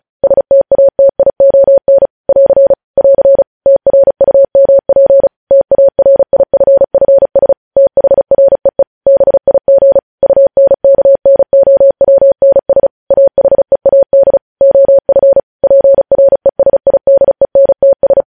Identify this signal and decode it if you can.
Morse code, heard as a keyed tone, about 36 wpm, 550 Hz. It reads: STATION <AR> <AR> TRUMP TARIFFS THREE BIG UNKNOWNS AHEAD OF PRESIDENTS
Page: news headlines in morse code at 35 wpm